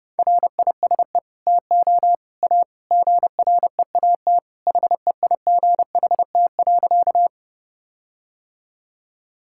Morse code, heard as a keyed tone, about 32 wpm, 700 Hz. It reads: RISE TO A GREAT HEIGHT.
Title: Sentences from Top 1600 Words 30wpm